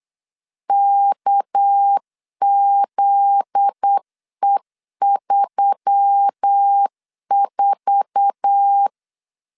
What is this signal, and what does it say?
Morse code, heard as a keyed tone, about 8 wpm, 800 Hz. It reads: KZE34
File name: LEZIONI / 8 YJ45 / 8 YJ4 8wpm 3246